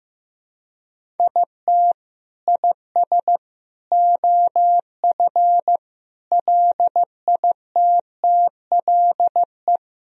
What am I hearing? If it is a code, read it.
Morse code, heard as a keyed tone, about 15 wpm, 700 Hz. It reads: IT IS OF LITTLE
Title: Sentences from Top 1000 Words - Review Part 1 - 15wpm